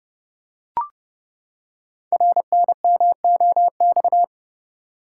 Morse code, heard as a keyed tone, about 28 wpm, 700 Hz. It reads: RNMOX
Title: Code Groups - 5 Letters - 30wpm